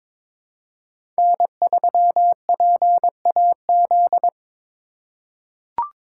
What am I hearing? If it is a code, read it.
Morse code, heard as a keyed tone, about 22 wpm, 700 Hz. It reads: N3PAZ
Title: USA Call Signs - 4x Character Spacing - 22wpm